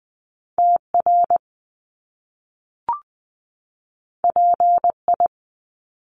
Morse code, heard as a keyed tone, about 20 wpm, 700 Hz. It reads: TR PI
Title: Top Two Letter Combinations in Words 20wpm